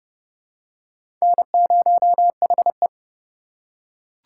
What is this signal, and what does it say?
Morse code, heard as a keyed tone, about 30 wpm, 700 Hz. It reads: N0HE
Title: USA Call Signs - 4 Characters - 4x Character Spacing - 30wpm